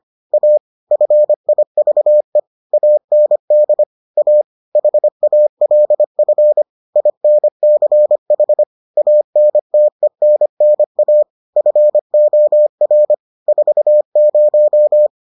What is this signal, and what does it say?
Morse code, heard as a keyed tone, about 25 wpm, 600 Hz. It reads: A FIVE AND A HALF INCH ANTENNA FOR 40